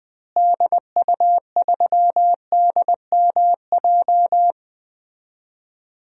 Morse code, heard as a keyed tone, about 20 wpm, 700 Hz. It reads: DU3DMJ
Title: DXCC-100 Call Signs - 6 Characters - Repeated 2x - 20wpm